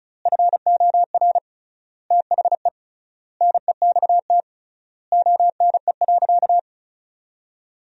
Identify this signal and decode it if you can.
Morse code, heard as a keyed tone, about 36 wpm, 700 Hz. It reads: FOR THE NEXT ONE.
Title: 3x Word-Spacing - Sentences from Top 400 Words - 35wpm